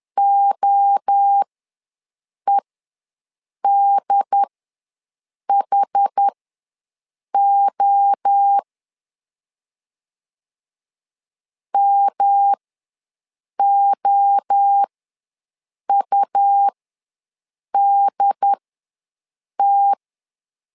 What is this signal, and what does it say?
Morse code, heard as a keyed tone, about 10 wpm, spaced as for 6 wpm, 800 Hz. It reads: OEDHO MOUDT
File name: LEZIONI / 3 MWRO / 3 MWRO ws 10wpm 2310